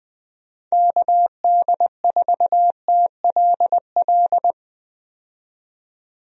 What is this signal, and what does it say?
Morse code, heard as a keyed tone, about 20 wpm, 700 Hz. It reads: KD4TLL
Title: USA Call Signs - 6 Characters - 20wpm